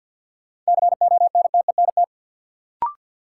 Morse code, heard as a keyed tone, about 49 wpm, 700 Hz. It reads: CONTENT
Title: Sentences from Top 900 Words - New Word Drill - 50wpm